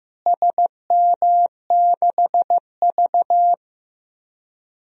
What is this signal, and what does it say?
Morse code, heard as a keyed tone, about 15 wpm, 700 Hz. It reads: SM6V
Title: DXCC-100 Call Signs - 4 Characters - Repeated 3x - 15wpm